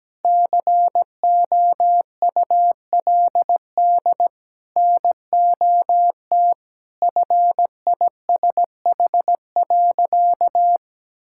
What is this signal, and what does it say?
Morse code, heard as a keyed tone, about 17 wpm, 700 Hz. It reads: COULD NOT FISH.
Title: Sentences from Top 1600 Words 17wpm